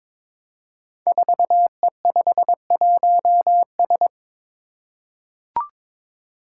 Morse code, heard as a keyed tone, about 22 wpm, 700 Hz. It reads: 4E51S
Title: Five Challenging Letter-Number Combinations 22wpm